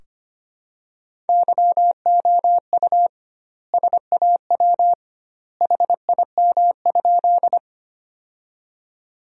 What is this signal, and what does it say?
Morse code, heard as a keyed tone, about 25 wpm, 700 Hz. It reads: YOU SAW HIM?
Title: 2x Word-Spacing - Sentences from Top 300 Words - 25wpm